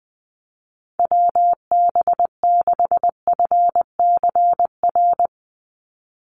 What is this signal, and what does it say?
Morse code, heard as a keyed tone, about 20 wpm, 700 Hz. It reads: WB6FCR
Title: USA Call Signs 20wpm